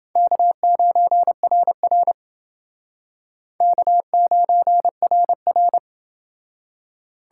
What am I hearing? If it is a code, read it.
Morse code, heard as a keyed tone, about 29 wpm, 700 Hz. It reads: K9RR K9RR